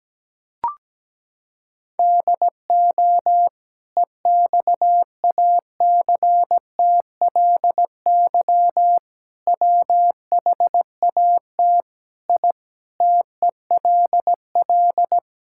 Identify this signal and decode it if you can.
Morse code, heard as a keyed tone, about 17 wpm, 700 Hz. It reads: DO EXACTLY WHAT I TELL
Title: Sentences from Top 1000 Words - Review Part 6 - 17wpm